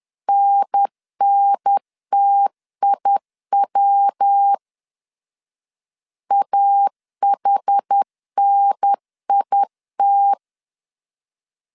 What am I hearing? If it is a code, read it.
Morse code, heard as a keyed tone, about 10 wpm, 800 Hz. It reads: NNTIW AHNIT
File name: CW / DET / 1 3 1500C S B 10wpm 3335